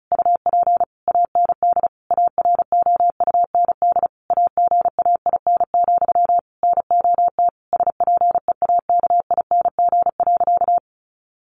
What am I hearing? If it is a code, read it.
Morse code, heard as a keyed tone, about 34 wpm, 700 Hz. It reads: UP AND AROUND AGAIN, NOT SPEAKING.
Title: A Sound Of Thunder 35wpm